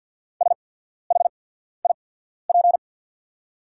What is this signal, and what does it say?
Morse code, heard as a keyed tone, about 48 wpm, spaced as for 14 wpm, 690 Hz. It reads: SHIP
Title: 8x Character Spacing - Top 500 Words - 50wpm